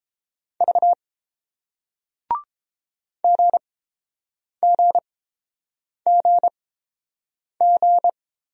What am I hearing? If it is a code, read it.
Morse code, heard as a keyed tone, about 32 wpm, spaced as for 7 wpm, 700 Hz. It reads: V GGGG